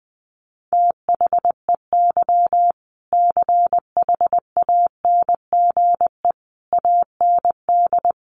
Morse code, heard as a keyed tone, about 19 wpm, 700 Hz. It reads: THEY CHANGE AND
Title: Sentences from Top 1200 Words 20wpm